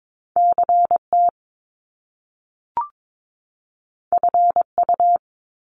Morse code, heard as a keyed tone, about 22 wpm, 700 Hz. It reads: CT FU
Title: Top Two Letter Combinations in Words 22wpm